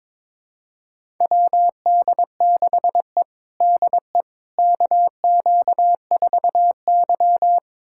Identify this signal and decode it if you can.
Morse code, heard as a keyed tone, about 21 wpm, 700 Hz. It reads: WD6E DE KQ4Y
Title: USA Callsign DE Callsign - 4 Characters - 22wpm